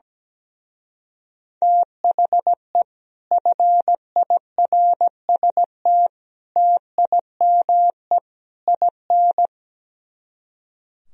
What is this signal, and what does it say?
Morse code, heard as a keyed tone, about 17 wpm, 700 Hz. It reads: THE FIRST TIME IN
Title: Speed-Racing - Sets of 4 Words 17wpm